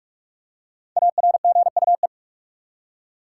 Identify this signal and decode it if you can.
Morse code, heard as a keyed tone, about 42 wpm, 700 Hz. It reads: ARGUE
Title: Five Letter Words - 45wpm